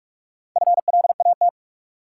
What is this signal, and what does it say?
Morse code, heard as a keyed tone, about 47 wpm, 700 Hz. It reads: FLAT
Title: Sentences from Top 1000 Words - New Word Drill - 45wpm